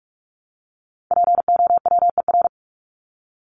Morse code, heard as a keyed tone, about 44 wpm, 700 Hz. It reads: POWER
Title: ES:S Suffix - Top 100 Words with Base Words 45wpm